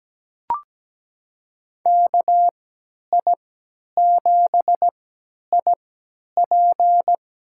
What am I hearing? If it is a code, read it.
Morse code, heard as a keyed tone, about 17 wpm, spaced as for 10 wpm, 700 Hz. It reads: KI7IP
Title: USA Call Signs - 3x Character Spacing - 17wpm